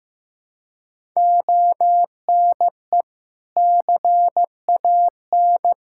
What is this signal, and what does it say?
Morse code, heard as a keyed tone, about 15 wpm, 700 Hz. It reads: ONE CAN
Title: Sentences from Top 1000 Words - Review Part 5 - 15wpm